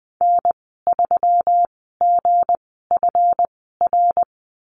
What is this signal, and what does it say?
Morse code, heard as a keyed tone, about 20 wpm, spaced as for 15 wpm, 700 Hz. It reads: N3GFR
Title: 179 Lesson 29g - Slash - Call Signs 20wpm